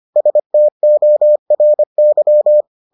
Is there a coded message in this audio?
Yes: Morse code from 0.2 to 2.6 s, heard as a keyed tone, about 25 wpm, 600 Hz. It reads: STORY